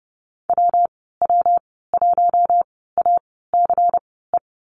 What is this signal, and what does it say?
Morse code, heard as a keyed tone, about 30 wpm, spaced as for 17 wpm, 700 Hz. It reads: WW1ACE
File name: Lesson 19g - Letter w - Call Signs 30wpm